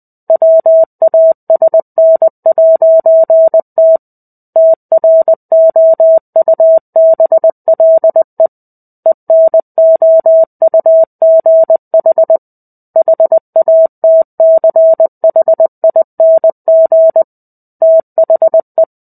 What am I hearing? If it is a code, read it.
Morse code, heard as a keyed tone, about 20 wpm, 650 Hz. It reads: WASN'T TROUBLE ENOUGH HATCHING THE
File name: 05-Chapter-05 20 WPM